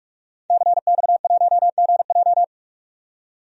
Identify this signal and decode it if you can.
Morse code, heard as a keyed tone, about 46 wpm, 700 Hz. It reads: CX1GJ
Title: DXCC-100 Call Signs - 5 Characters - 45wpm